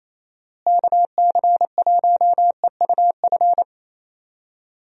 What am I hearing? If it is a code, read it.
Morse code, heard as a keyed tone, about 28 wpm, 700 Hz. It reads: KC1EUF
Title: USA Call Signs - 6 Characters - Repeated 2x - 28wpm